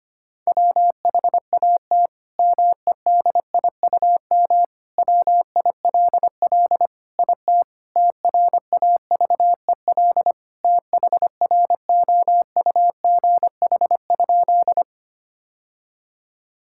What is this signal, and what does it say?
Morse code, heard as a keyed tone, about 25 wpm, 700 Hz. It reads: WHAT MEDIUM WILL IT TRAVEL THROUGH?
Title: Sentences from Top 1000 Words - Review Part 6 - 25wpm